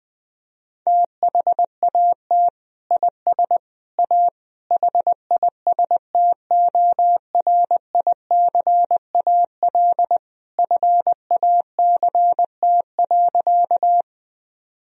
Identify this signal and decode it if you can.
Morse code, heard as a keyed tone, about 20 wpm, 700 Hz. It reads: THAT IS A HISTORICAL FACT.
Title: Sentences from Top 1000 Words - Review Part 4 - 20wpm